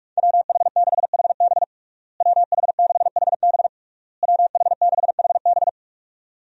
Morse code, heard as a keyed tone, about 46 wpm, 700 Hz. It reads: WH6HB WH6HB WH6HB
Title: USA Call Signs - 5 Characters - Repeated 3x - 45wpm